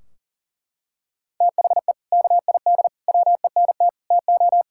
Morse code, heard as a keyed tone, about 40 wpm, 700 Hz. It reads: THE KID WENT TO
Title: Sentences from Top 1600 Words 40wpm